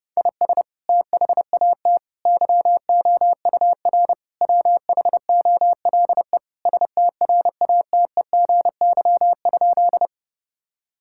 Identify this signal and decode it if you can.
Morse code, heard as a keyed tone, about 32 wpm, 700 Hz. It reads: IS THAT YOUR WHOLE STRATEGY?
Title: Sentences from Top 1000 Words - Review Part 5 - 30wpm